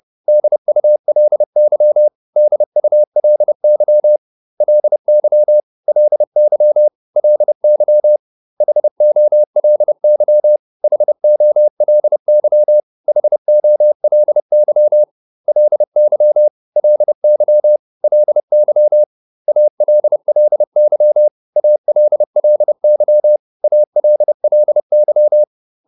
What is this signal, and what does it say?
Morse code, heard as a keyed tone, about 30 wpm, 600 Hz. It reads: DULY DULY LY LY LY HOLY HOLY HOLY LY LY LY ALLY ALLY ALLY